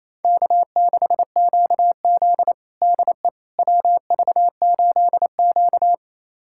Morse code, heard as a keyed tone, about 28 wpm, 700 Hz. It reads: K6QZ DE WV8Q
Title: USA Callsign DE Callsign - 4 Characters - 28wpm